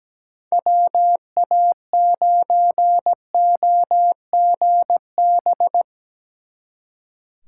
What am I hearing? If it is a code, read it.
Morse code, heard as a keyed tone, about 19 wpm, 700 Hz. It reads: WA9OGB
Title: Speed Racing - USA 2x3 Call Signs 17wpm